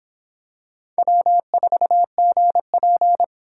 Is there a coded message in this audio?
Yes: Morse code from 1.0 to 3.2 s, heard as a keyed tone, about 26 wpm, 700 Hz. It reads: W4GP